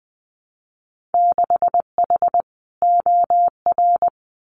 Morse code, heard as a keyed tone, about 20 wpm, 700 Hz. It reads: 6H OR
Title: Winter Field Day Exchanges Encore 20wpm